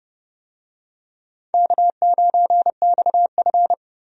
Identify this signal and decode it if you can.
Morse code, heard as a keyed tone, about 30 wpm, 700 Hz. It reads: K9XF
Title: USA Call Signs - 4 Characters - 30wpm